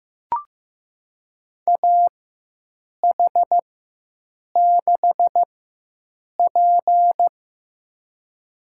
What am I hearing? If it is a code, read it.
Morse code, heard as a keyed tone, about 16 wpm, spaced as for 7 wpm, 700 Hz. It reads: AH6P
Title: USA Call Signs - 4 Characters - 4x Character Spacing - 15wpm